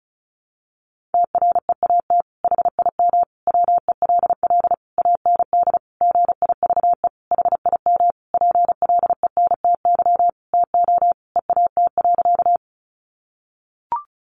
Morse code, heard as a keyed tone, about 34 wpm, 700 Hz. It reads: TREAT HIM WELL AND GIVE HIM PLENTY TO EAT.
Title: Sentences from Top 1300 Words 35wpm